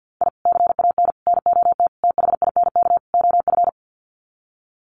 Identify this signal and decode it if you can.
Morse code, heard as a keyed tone, about 50 wpm, 700 Hz. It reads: I CAN NOT THINK OF
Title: Sets of 5 Words 50wpm